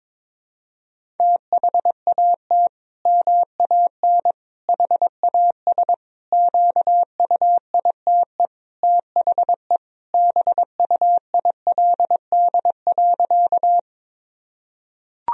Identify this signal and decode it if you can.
Morse code, heard as a keyed tone, about 22 wpm, 700 Hz. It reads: THAT MAN HAS QUITE THE BUILD.
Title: Sentences from Top 1000 Words - Review Part 1 - 22wpm